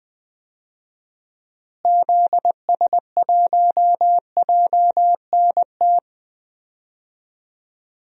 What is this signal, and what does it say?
Morse code, heard as a keyed tone, about 20 wpm, 700 Hz. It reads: ZS1JNT
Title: DXCC-100 Call Signs - 6 Characters - Repeated 3x - 20wpm